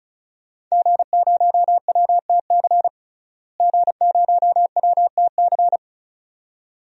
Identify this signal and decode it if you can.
Morse code, heard as a keyed tone, about 36 wpm, 700 Hz. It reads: G0WTC G0WTC
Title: DXCC-100 Call Signs - 5 Characters - Repeated 2x - 35wpm